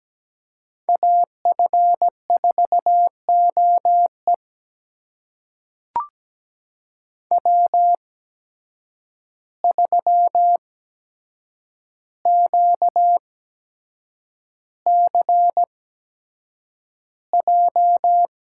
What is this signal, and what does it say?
Morse code, heard as a keyed tone, about 17 wpm, spaced as for 7 wpm, 700 Hz. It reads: AF4OE W3QCJ
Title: USA Call Signs - 5 Characters - 8x Character Spacing - 17wpm